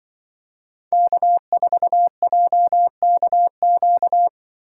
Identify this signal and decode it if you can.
Morse code, heard as a keyed tone, about 23 wpm, 700 Hz. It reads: K4JKQ
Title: Speed Racing - USA 1x3 Call Signs 20wpm